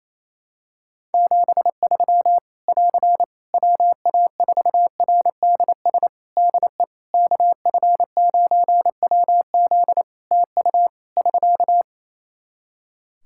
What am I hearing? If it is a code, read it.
Morse code, heard as a keyed tone, about 28 wpm, 700 Hz. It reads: 73 <AR> WA4RDS DE KF9WZ TU <SK>
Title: K1USN SST QSO Practice 28wpm